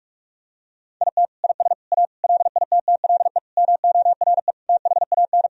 Morse code, heard as a keyed tone, about 46 wpm, 700 Hz. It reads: IT IS A LITTLE MORE THAN
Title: Sentences from Top 1000 Words - Review Part 1 - 45wpm